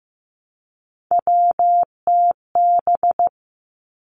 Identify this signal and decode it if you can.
Morse code, heard as a keyed tone, about 15 wpm, 700 Hz. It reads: WTB
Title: Three Challenging Letter-Number Combinations 15wpm